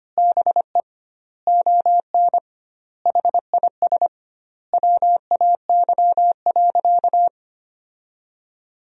Morse code, heard as a keyed tone, about 25 wpm, 700 Hz. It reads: BE ON HIS WAY.
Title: 2x Word-Spacing - Sentences from Top 300 Words - 25wpm